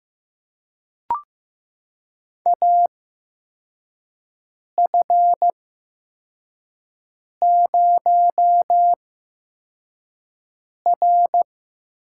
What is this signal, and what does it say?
Morse code, heard as a keyed tone, about 15 wpm, spaced as for 4 wpm, 700 Hz. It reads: AF0R
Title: USA Call Signs - 4 Characters - 8x Character Spacing - 15wpm